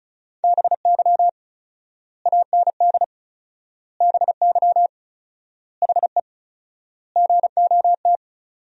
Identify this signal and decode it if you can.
Morse code, heard as a keyed tone, about 36 wpm, 700 Hz. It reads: BY AND BY HE GOT
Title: 4x Word-Spacing - Sentences from Top 300 Words - 35wpm